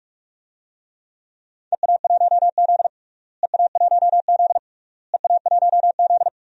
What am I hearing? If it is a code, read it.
Morse code, heard as a keyed tone, about 44 wpm, 700 Hz. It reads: EA1Z EA1Z EA1Z
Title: DXCC-100 Call Signs - 4 Characters - Repeated 3x - 45wpm